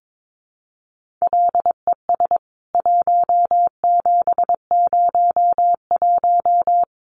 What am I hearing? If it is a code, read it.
Morse code, heard as a keyed tone, about 22 wpm, 700 Hz. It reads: LES 1701
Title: CWT Exchange 22wpm